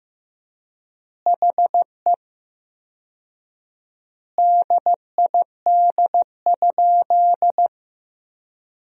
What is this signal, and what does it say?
Morse code, heard as a keyed tone, about 15 wpm, 700 Hz. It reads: HE DID?
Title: 4x Word-Spacing - Sentences from Top 100 Words - 15wpm